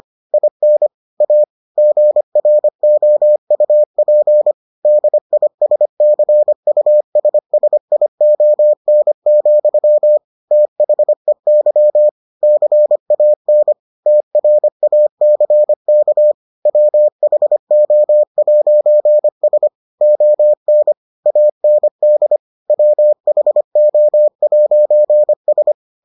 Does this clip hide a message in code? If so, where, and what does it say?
Yes: Morse code from 0.3 to 25.7 s, heard as a keyed tone, about 25 wpm, 600 Hz. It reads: IN A GROUP DISCUSSION, THEY CAN TRACK WHO'S ON AND WHO'S